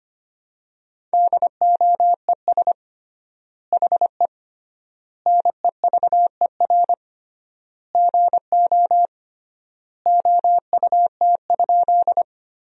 Morse code, heard as a keyed tone, about 25 wpm, 700 Hz. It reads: DOES HE NEVER GO OUT?
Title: 3x Word-Spacing - Sentences from Top 300 Words - 25wpm